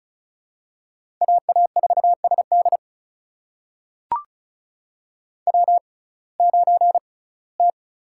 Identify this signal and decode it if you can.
Morse code, heard as a keyed tone, about 36 wpm, spaced as for 21 wpm, 700 Hz. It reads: AA4SD W9T
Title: USA Call Signs - 5 Characters - 6x Character Spacing - 35wpm